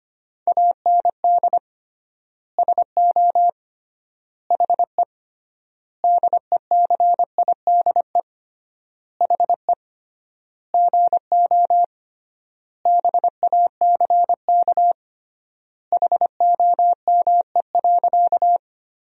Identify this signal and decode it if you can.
Morse code, heard as a keyed tone, about 25 wpm, 700 Hz. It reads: AND SO HE DECIDE HE GO BACK HOME.
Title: 3x Word-Spacing - Sentences from Top 500 Words - 25wpm